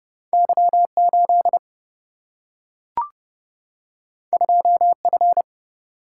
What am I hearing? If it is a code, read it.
Morse code, heard as a keyed tone, about 30 wpm, 700 Hz. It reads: Y 8 2 F
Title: Single-Letter Single-Number 30wpm